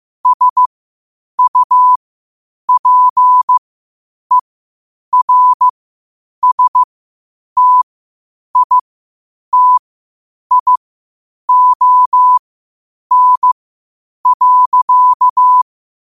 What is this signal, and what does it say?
Morse code, heard as a keyed tone, about 15 wpm, spaced as for 8 wpm, 1000 Hz. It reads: SUPERSTITION.
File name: qotdfiles / 8wpm